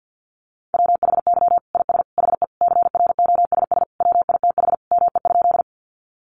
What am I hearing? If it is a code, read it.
Morse code, heard as a keyed tone, about 50 wpm, 700 Hz. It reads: WHY IS HE CROSS WITH ME?
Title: Sentences from Top 300 Words 50wpm